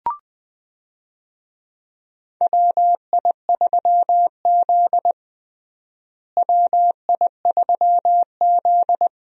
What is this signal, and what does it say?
Morse code, heard as a keyed tone, about 20 wpm, 700 Hz. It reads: WI3Z WI3Z
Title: USA Call Signs - 4 Characters - Repeated 3x - 20wpm